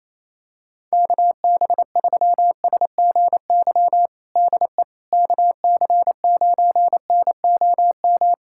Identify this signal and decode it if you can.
Morse code, heard as a keyed tone, about 28 wpm, 700 Hz. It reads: KB3SGY DE KC9NOM
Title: USA Callsign DE Callsign - 6 Characters - 28wpm